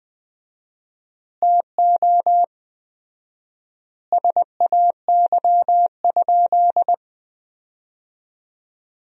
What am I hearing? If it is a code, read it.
Morse code, heard as a keyed tone, about 20 wpm, 700 Hz. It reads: TO SAY?
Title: 4x Word-Spacing - Sentences from Top 200 Words - 20wpm